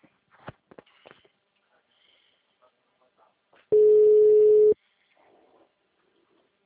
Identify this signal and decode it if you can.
Ringback tone (Europe, 425 Hz).